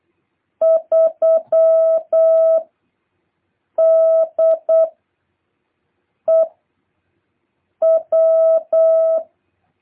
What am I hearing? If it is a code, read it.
Morse code, heard as a keyed tone, about 8 wpm, 650 Hz. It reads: 3 D E W
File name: jh8cbh / 2023cw